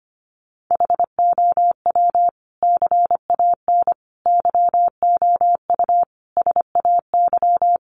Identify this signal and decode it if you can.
Morse code, heard as a keyed tone, about 25 wpm, 700 Hz. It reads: HOW CAN YOU SAY
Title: Sentences from Top 200 Words 25wpm